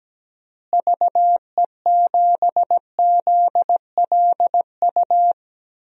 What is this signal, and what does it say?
Morse code, heard as a keyed tone, about 17 wpm, 700 Hz. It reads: VE7ZLU
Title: DXCC-100 Call Signs - 6 Characters - Repeated 3x - 17wpm